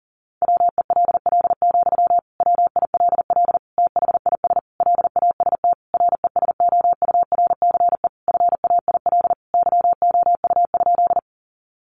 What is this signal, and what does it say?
Morse code, heard as a keyed tone, about 40 wpm, 700 Hz. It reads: WELL, WILL THIS LAST RESOURCE FAIL YOU?
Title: Sentences from Top 1300 Words 40wpm